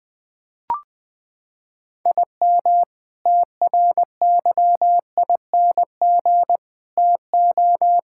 Morse code, heard as a keyed tone, about 20 wpm, 700 Hz. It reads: IM TRYING TO
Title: Sentences with 4-Word Phrases - 20wpm